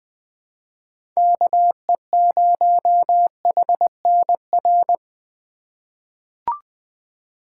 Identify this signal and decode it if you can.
Morse code, heard as a keyed tone, about 20 wpm, 700 Hz. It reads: KE0HNR
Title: USA Call Signs - 3x Character Spacing - 20wpm